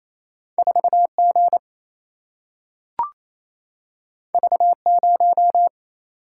Morse code, heard as a keyed tone, about 28 wpm, 700 Hz. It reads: 4G V0
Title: Single-Letter Single-Number 28wpm